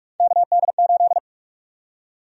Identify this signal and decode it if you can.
Morse code, heard as a keyed tone, about 46 wpm, 700 Hz. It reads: KD8
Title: USA Call Sign Prefixes - 45wpm